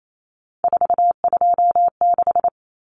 Morse code, heard as a keyed tone, about 28 wpm, 700 Hz. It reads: 426
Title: Common Cut Numbers - 3-Digit 28wpm